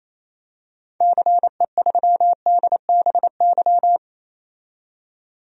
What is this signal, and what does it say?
Morse code, heard as a keyed tone, about 28 wpm, 700 Hz. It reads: CE3DBY
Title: DXCC-100 Call Signs - 6 Characters - 28wpm